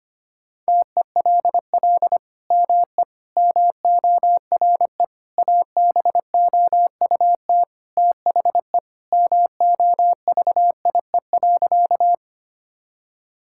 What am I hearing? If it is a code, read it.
Morse code, heard as a keyed tone, about 25 wpm, 700 Hz. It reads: TELL ME MORE ABOUT THE MOVIE.
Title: Sentences from Top 1000 Words - Review Part 5 - 25wpm